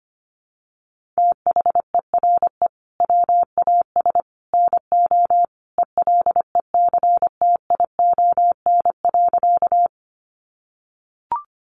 Morse code, heard as a keyed tone, about 25 wpm, 700 Hz. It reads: THERE WAS NO ELECTION.
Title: Sentences from Top 1500 Words 25wpm